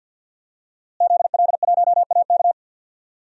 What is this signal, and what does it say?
Morse code, heard as a keyed tone, about 51 wpm, 700 Hz. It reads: ZL1AK
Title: DXCC-100 Call Signs - 5 Characters - Repeated 3x - 50wpm